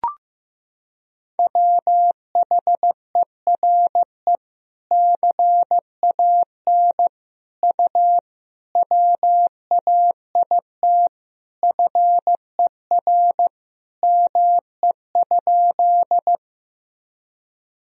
Morse code, heard as a keyed tone, about 15 wpm, 700 Hz. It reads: WHERE CAN U WAIT FER ME?
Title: Sentences from Top 500 Words and Abbreviations 15wpm